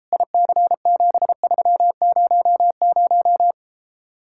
Morse code, heard as a keyed tone, about 33 wpm, 700 Hz. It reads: IC7300